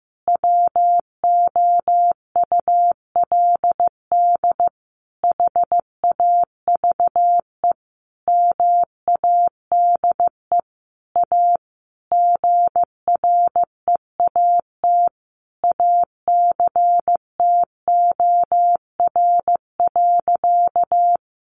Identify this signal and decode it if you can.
Morse code, heard as a keyed tone, about 15 wpm, 700 Hz. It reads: WOULD HAVE MADE A GREAT ACTOR.
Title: Sentences from Top 1500 Words 15wpm